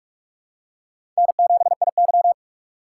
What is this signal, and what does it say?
Morse code, heard as a keyed tone, about 46 wpm, 700 Hz. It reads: N7IY
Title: USA Call Signs - 4 Characters - 3x Character Spacing - 45wpm